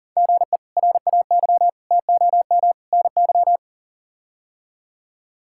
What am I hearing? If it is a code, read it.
Morse code, heard as a keyed tone, about 37 wpm, 700 Hz. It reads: GE RAY TOM NY
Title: K1USN SST QSO Practice - 40wpm